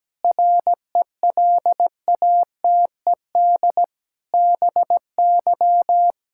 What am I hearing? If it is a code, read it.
Morse code, heard as a keyed tone, about 17 wpm, 700 Hz. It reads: RELATED BY
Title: Sentences from Top 1000 Words - Review Part 5 - 17wpm